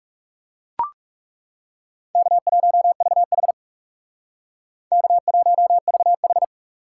Morse code, heard as a keyed tone, about 35 wpm, 700 Hz. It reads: K1VH K1VH
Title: Speed Racing - USA 1x2 Call Signs 30wpm